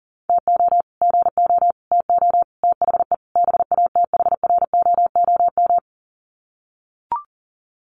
Morse code, heard as a keyed tone, about 37 wpm, 700 Hz. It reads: TO GO TO THE BATHROOM
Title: Sets of 5 Words Encore 40wpm